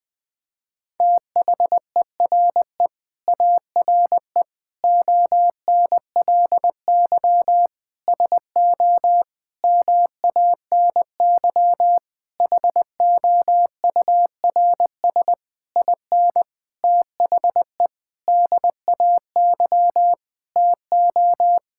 Morse code, heard as a keyed tone, about 20 wpm, 700 Hz. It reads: THERE ARE ONLY SO MANY HOURS IN THE DAY TO
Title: Sentences with 9-Word Phrases - 20wpm